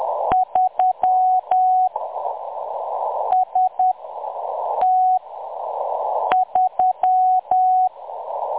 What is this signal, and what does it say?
Morse code, heard as a keyed tone, about 10 wpm, spaced as for 6 wpm, 740 Hz. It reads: 3 ST3
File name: M23-ST3- -0858utc-10184khz